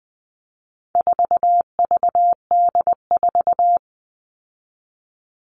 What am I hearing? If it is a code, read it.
Morse code, heard as a keyed tone, about 20 wpm, 700 Hz. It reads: 4VD4
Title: All Cut Numbers - 4-Digit 20wpm